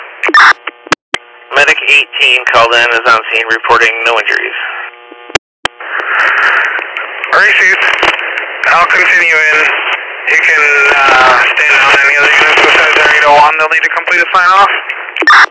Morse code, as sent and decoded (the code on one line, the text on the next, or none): none